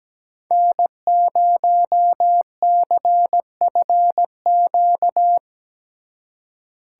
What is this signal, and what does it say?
Morse code, heard as a keyed tone, about 20 wpm, 700 Hz. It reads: N0CFQ
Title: Speed Racing - USA 1x3 Call Signs 17wpm